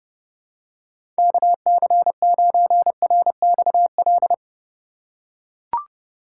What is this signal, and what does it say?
Morse code, heard as a keyed tone, about 31 wpm, 700 Hz. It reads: KC9RXL
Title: USA Call Signs - 6 Characters - 2x Character Spacing - 30wpm